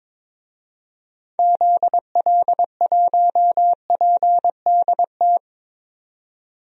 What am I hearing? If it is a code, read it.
Morse code, heard as a keyed tone, about 22 wpm, 700 Hz. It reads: ZL1PDT
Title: DXCC-100 Call Signs - 6 Characters - 22wpm